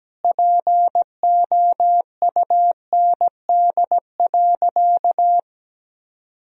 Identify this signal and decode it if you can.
Morse code, heard as a keyed tone, about 17 wpm, 700 Hz. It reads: POUND.
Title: Sentences from Top 1000 Words - Review Part 2 - 17wpm